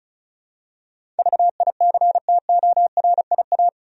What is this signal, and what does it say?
Morse code, heard as a keyed tone, about 36 wpm, 700 Hz. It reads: VICTORIA
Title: USA Common Names - 35wpm